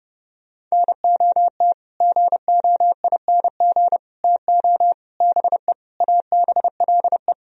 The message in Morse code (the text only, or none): NOT GOING TO BE ABLE